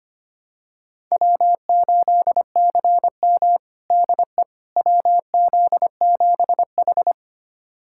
Morse code, heard as a keyed tone, about 25 wpm, 700 Hz. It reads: W8CM DE WZ7H
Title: USA Callsign DE Callsign - 4 Characters - 25wpm